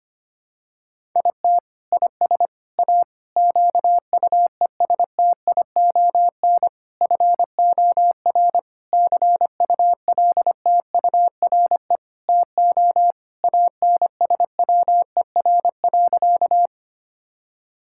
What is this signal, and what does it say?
Morse code, heard as a keyed tone, about 25 wpm, 700 Hz. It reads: IT IS A QUESTION FOR CULTURE TO ANSWER.
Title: Sentences from Top 1000 Words - Review Part 6 - 25wpm